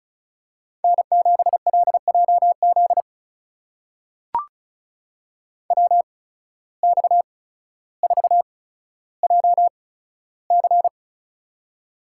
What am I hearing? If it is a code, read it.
Morse code, heard as a keyed tone, about 34 wpm, spaced as for 15 wpm, 700 Hz. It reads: N7LJZ WX4JC
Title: USA Call Signs - 5 Characters - 8x Character Spacing - 35wpm